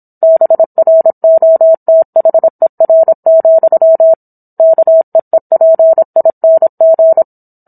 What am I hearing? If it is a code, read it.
Morse code, heard as a keyed tone, about 27 wpm, 650 Hz. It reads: BROTHER, KEEPING